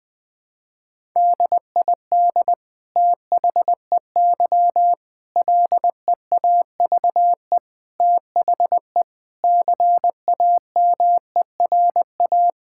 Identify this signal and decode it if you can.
Morse code, heard as a keyed tone, about 20 wpm, 700 Hz. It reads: DID THEY LEAVE THE CAMERA
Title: Sentences from Top 1000 Words - Review Part 5 - 20wpm